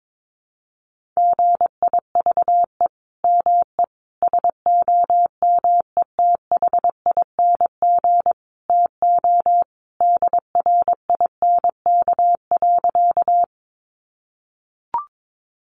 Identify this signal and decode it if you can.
Morse code, heard as a keyed tone, about 22 wpm, 700 Hz. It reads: GIVE ME SOMETHING TO DRINK.
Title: Sentences from Top 1500 Words 22wpm